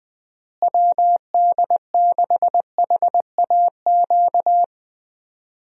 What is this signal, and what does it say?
Morse code, heard as a keyed tone, about 20 wpm, 700 Hz. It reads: WD6HAQ
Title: USA Call Signs - 6 Characters - Repeated 2x - 20wpm